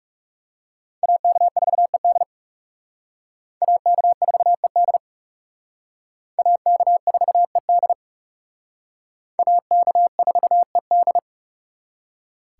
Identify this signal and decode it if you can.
Morse code, heard as a keyed tone, about 37 wpm, 700 Hz. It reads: AK4ED AK4ED AK4ED AK4ED
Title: Speed Racing - USA 2x2 Call Signs 30wpm